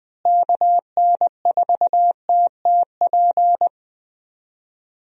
Morse code, heard as a keyed tone, about 21 wpm, 700 Hz. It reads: KN4TTP
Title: USA Call Signs - 6 Characters - Repeated 2x - 20wpm